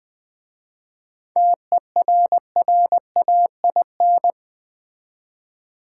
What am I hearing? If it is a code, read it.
Morse code, heard as a keyed tone, about 20 wpm, 700 Hz. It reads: TERRAIN
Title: 7 Letter Words - First 3 Letters Given - 20wpm